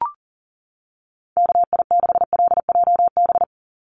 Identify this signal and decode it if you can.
Morse code, heard as a keyed tone, about 40 wpm, 700 Hz. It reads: KI6LJB
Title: USA Call Signs 40wpm